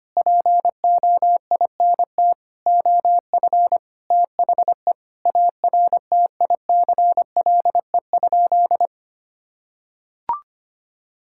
Morse code, heard as a keyed tone, about 25 wpm, 700 Hz. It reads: POINT OF THE ARTICLE?
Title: Sentences from Top 1000 Words - Review Part 5 - 25wpm